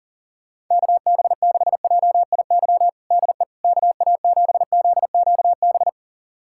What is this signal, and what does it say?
Morse code, heard as a keyed tone, about 40 wpm, 690 Hz. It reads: KB6JIY DE KA7ZQB
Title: USA Callsign DE Callsign - 6 Characters - 40wpm